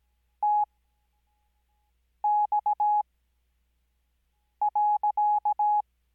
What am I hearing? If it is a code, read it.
Morse code, heard as a keyed tone, about 17 wpm, spaced as for 5 wpm, 840 Hz. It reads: TX.
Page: Farnsworth Spacing: 15WPM Character speed Spaced at 5WPM.